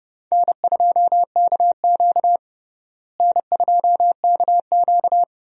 Morse code, heard as a keyed tone, about 30 wpm, 700 Hz. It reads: N2KQ N2KQ
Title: USA Call Signs - 4 Characters - Repeated 2x - 30wpm